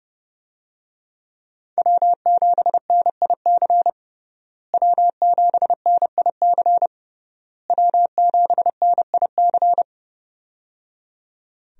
Morse code, heard as a keyed tone, about 30 wpm, 700 Hz. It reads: W7NIC W7NIC W7NIC
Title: USA Call Signs - 5 Characters - Repeated 3x - 30wpm